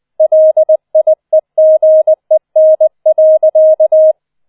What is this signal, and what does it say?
Morse code, heard as a keyed tone, about 20 wpm, 630 Hz. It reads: LIEGEN.